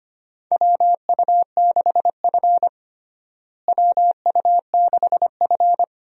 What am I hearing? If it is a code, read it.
Morse code, heard as a keyed tone, about 25 wpm, 700 Hz. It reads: WU6F WU6F
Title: USA Call Signs - 4 Characters - Repeated 3x - 25wpm